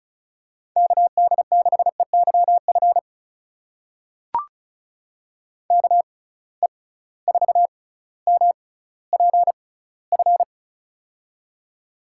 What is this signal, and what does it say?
Morse code, heard as a keyed tone, about 36 wpm, spaced as for 18 wpm, 690 Hz. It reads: KD6EYF KE4MPF
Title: USA Call Signs - 6 Characters - 6x Character Spacing - 35wpm